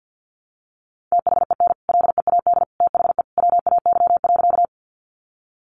Morse code, heard as a keyed tone, about 50 wpm, 700 Hz. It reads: THEN LEAD THE WAY.
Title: Sentences from Top 500 Words 50wpm